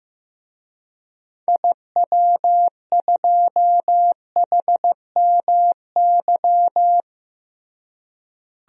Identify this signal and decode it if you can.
Morse code, heard as a keyed tone, about 15 wpm, 700 Hz. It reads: IW2HMY
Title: DXCC-100 Call Signs - 6 Characters - Repeated 2x - 15wpm